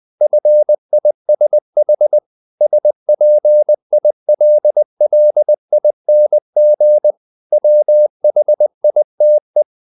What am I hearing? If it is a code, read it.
Morse code, heard as a keyed tone, about 20 wpm, 600 Hz. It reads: FISH SPILLING WHITE
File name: the old man and the sea part 2 20wpm